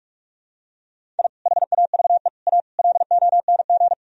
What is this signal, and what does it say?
Morse code, heard as a keyed tone, about 41 wpm, 700 Hz. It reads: I HAVE A LONG
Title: Sentences from Top 1000 Words - Review Part 1 - 45wpm